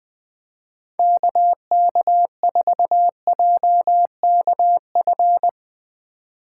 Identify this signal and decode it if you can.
Morse code, heard as a keyed tone, about 20 wpm, 700 Hz. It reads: KK4JKF
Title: USA Call Signs - 3x Character Spacing - 20wpm